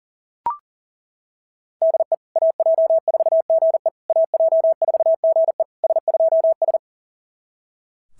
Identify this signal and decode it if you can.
Morse code, heard as a keyed tone, about 40 wpm, 690 Hz. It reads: DE AJ4GE AJ4GE S2S
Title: SOTA QSO Practice 40wpm